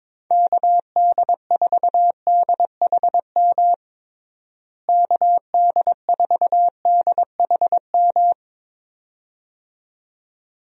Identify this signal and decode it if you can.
Morse code, heard as a keyed tone, about 22 wpm, 700 Hz. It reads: KD4DHM KD4DHM
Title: USA Call Signs - 6 Characters - Repeated 2x - 22wpm